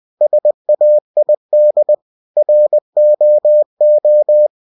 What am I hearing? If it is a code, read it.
Morse code, heard as a keyed tone, about 20 wpm, 600 Hz. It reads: SAID ROO
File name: cw-books / winnie-the-pooh / winnie-20wpm-page-0107